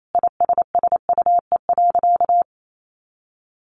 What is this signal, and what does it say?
Morse code, heard as a keyed tone, about 28 wpm, 700 Hz. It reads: ISSUE.
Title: Sentences from Top 900 Words 28wpm